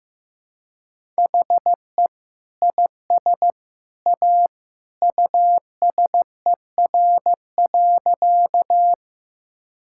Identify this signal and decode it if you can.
Morse code, heard as a keyed tone, about 15 wpm, 700 Hz. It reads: HE IS A USER.
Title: Sentences from Top 1000 Words - Review Part 5 - 15wpm